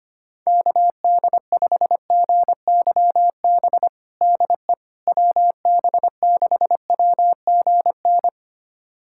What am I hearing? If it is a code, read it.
Morse code, heard as a keyed tone, about 25 wpm, 700 Hz. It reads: KD5GYB DE WB6WGN
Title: USA Callsign DE Callsign - 6 Characters - 25wpm